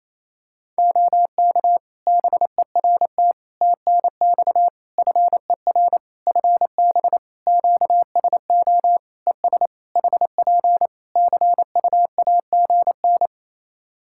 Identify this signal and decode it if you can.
Morse code, heard as a keyed tone, about 28 wpm, 700 Hz. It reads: OK BERT TNX FER FB QSO ES HP CUAGN
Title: Practice QSO - Exchange 4 - Ending - 28wpm